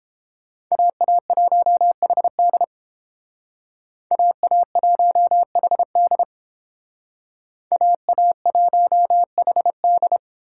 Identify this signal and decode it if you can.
Morse code, heard as a keyed tone, about 31 wpm, 700 Hz. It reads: AA1HD AA1HD AA1HD